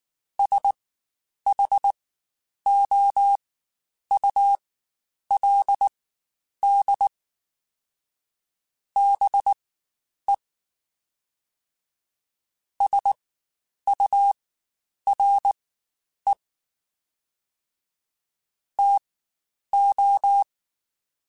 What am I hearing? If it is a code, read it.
Morse code, heard as a keyed tone, about 19 wpm, spaced as for 9 wpm, 800 Hz. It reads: SHOULD BE SURE TO
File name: Spaced / Chuckles Spaced-mp3s / BrainMouth sp @18